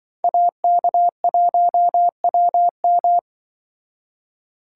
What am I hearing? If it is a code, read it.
Morse code, heard as a keyed tone, about 25 wpm, 700 Hz. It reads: AK1WM
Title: Speed Racing - USA 2x2 Call Signs 20wpm